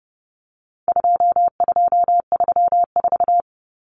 Morse code, heard as a keyed tone, about 30 wpm, 700 Hz. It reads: 2234
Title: Common Cut Numbers - 4-Digit 30wpm